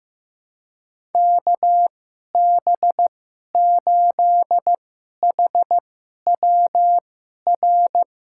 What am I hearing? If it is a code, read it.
Morse code, heard as a keyed tone, about 15 wpm, spaced as for 11 wpm, 700 Hz. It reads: KB8HWR
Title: USA Call Signs - 6 Characters - 2x Character Spacing - 15wpm